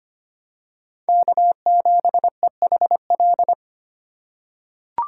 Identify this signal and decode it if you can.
Morse code, heard as a keyed tone, about 25 wpm, 700 Hz. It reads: K7EHL
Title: USA Call Signs - 5 Characters - 3x Character Spacing - 25wpm